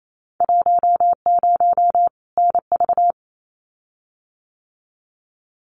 Morse code, heard as a keyed tone, about 28 wpm, 700 Hz. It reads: 10 NV
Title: Winter Field Day Exchanges 28wpm